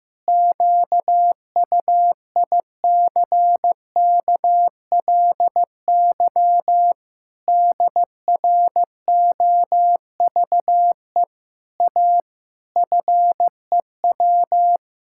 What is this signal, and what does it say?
Morse code, heard as a keyed tone, about 16 wpm, 700 Hz. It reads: QUICKLY DROVE A FEW
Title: Pangram Challenge v2 - 15wpm